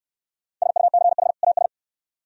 Morse code, heard as a keyed tone, about 64 wpm, 690 Hz. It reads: SUCH AS